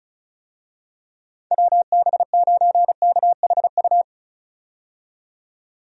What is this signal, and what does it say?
Morse code, heard as a keyed tone, about 34 wpm, 700 Hz. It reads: WB9KHU
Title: USA Call Signs - 6 Characters - 35wpm